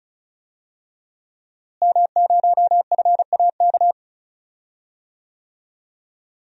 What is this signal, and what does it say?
Morse code, heard as a keyed tone, about 36 wpm, 700 Hz. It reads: M0FAK
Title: DXCC-100 Call Signs - 5 Characters - Repeated 3x - 35wpm